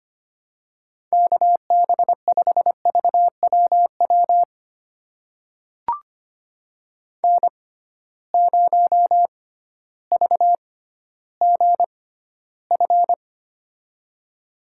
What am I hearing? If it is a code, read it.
Morse code, heard as a keyed tone, about 25 wpm, spaced as for 13 wpm, 700 Hz. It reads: KB5VWW N0VGF
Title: USA Call Signs - 6x Character Spacing - 25wpm